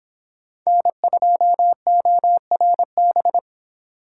Morse code, heard as a keyed tone, about 25 wpm, 700 Hz. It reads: N2ORB